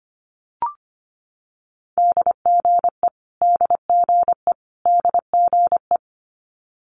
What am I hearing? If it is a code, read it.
Morse code, heard as a keyed tone, about 25 wpm, 700 Hz. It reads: DGE DGE DGE
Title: Multi-letter Phonemes Repeated 3x 25wpm